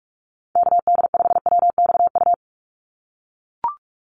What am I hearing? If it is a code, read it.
Morse code, heard as a keyed tone, about 44 wpm, 700 Hz. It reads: KD5WXU
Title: USA Call Signs 45wpm